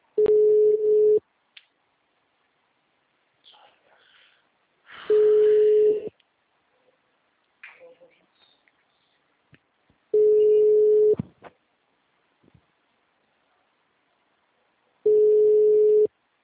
Ringback tone (Europe, 425 Hz).